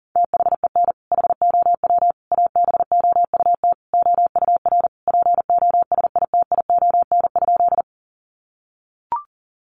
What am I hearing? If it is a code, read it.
Morse code, heard as a keyed tone, about 40 wpm, 700 Hz. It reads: THEN HOW ABOUT OUR POSITION?
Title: Sentences from Top 900 Words 40wpm